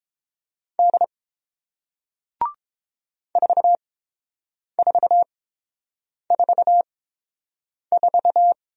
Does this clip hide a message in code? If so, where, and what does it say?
Yes: Morse code from 0.8 to 8.5 s, heard as a keyed tone, about 30 wpm, spaced as for 7 wpm, 700 Hz. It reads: D 4444